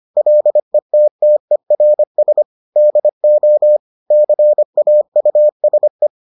Morse code, heard as a keyed tone, about 25 wpm, 600 Hz. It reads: LETTERS DO CAUSE